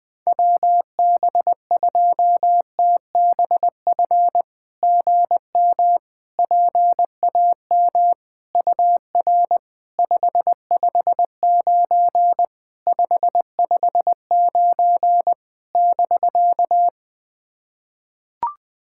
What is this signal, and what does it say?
Morse code, heard as a keyed tone, about 20 wpm, 700 Hz. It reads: WB2TBF GM PAM UR 559 559 <BK>
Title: SOTA QSO Practice 20wpm